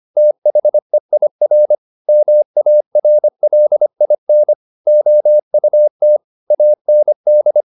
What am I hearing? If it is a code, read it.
Morse code, heard as a keyed tone, about 25 wpm, 600 Hz. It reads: THEIR MARLIN OUT AND
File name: the old man and the sea part 1 25wpm